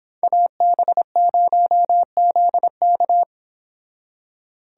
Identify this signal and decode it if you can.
Morse code, heard as a keyed tone, about 26 wpm, 700 Hz. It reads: AB0ZK
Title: Speed Racing - USA 2x2 Call Signs 22wpm